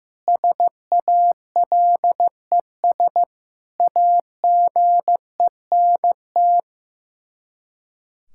Morse code, heard as a keyed tone, about 15 wpm, 700 Hz. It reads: SALES AGENT
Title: Practice QSO - Exchange 3 - More Info - 15wpm